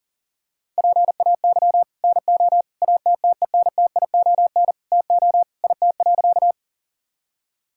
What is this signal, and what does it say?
Morse code, heard as a keyed tone, about 40 wpm, 700 Hz. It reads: PAY NO ATTENTION TO IT.
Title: Sentences from Top 1000 Words - Review Part 6 - 40wpm